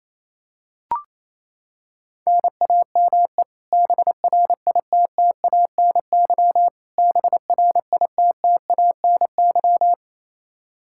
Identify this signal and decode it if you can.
Morse code, heard as a keyed tone, about 28 wpm, 700 Hz. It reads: NAME BRITTANY BRITTANY
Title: Practice QSO - Exchange 1 - Beginning - 28wpm